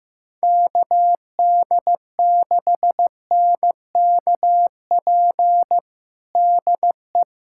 Morse code, heard as a keyed tone, about 15 wpm, 700 Hz. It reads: KD6NKP DE
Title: USA Callsign DE Callsign 15wpm